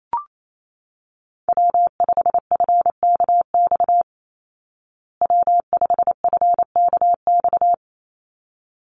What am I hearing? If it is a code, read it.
Morse code, heard as a keyed tone, about 28 wpm, 700 Hz. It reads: W5FKX W5FKX
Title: Call Signs Repeated 3x - 28wpm